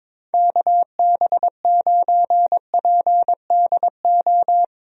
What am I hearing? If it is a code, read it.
Morse code, heard as a keyed tone, about 22 wpm, 700 Hz. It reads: KB9PDO
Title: USA Call Signs - 6 Characters - Repeated 3x - 22wpm